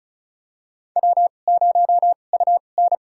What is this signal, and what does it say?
Morse code, heard as a keyed tone, about 36 wpm, spaced as for 25 wpm, 700 Hz. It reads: W0UN
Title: USA Call Signs - 4 Characters - 2x Character Spacing - 35wpm